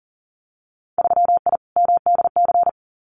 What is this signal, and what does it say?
Morse code, heard as a keyed tone, about 40 wpm, 700 Hz. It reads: 3I MDC
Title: Winter Field Day Exchanges Encore 40wpm